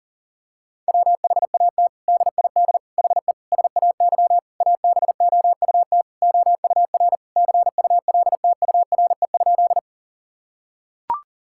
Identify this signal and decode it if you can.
Morse code, heard as a keyed tone, about 37 wpm, 700 Hz. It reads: WHAT DID HE SAY ABOUT OUR CULTURE?
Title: Sentences from Top 1000 Words - Review Part 6 - 40wpm